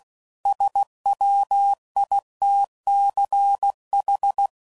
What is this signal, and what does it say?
Morse code, heard as a keyed tone, about 16 wpm, 800 Hz. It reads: SWITCH